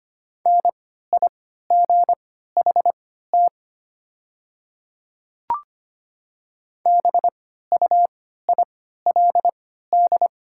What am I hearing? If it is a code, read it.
Morse code, heard as a keyed tone, about 25 wpm, spaced as for 14 wpm, 700 Hz. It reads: NIGHT BUILD
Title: 3x Character Spacing - Top 500 Words - 25wpm